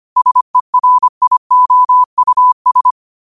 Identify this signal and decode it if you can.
Morse code, heard as a keyed tone, about 25 wpm, 1000 Hz. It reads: SERIOUS
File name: qotdfiles / 25wpm